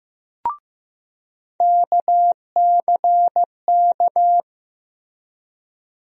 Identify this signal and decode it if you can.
Morse code, heard as a keyed tone, about 16 wpm, 700 Hz. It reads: KCK
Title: USA Call Sign Postfixes - 3 Letters 15wpm